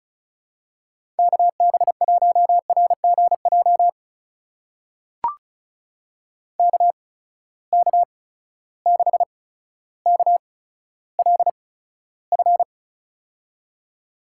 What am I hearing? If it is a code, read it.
Morse code, heard as a keyed tone, about 36 wpm, spaced as for 15 wpm, 700 Hz. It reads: KB1RGJ KK6KLF
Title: USA Call Signs - 6 Characters - 8x Character Spacing - 35wpm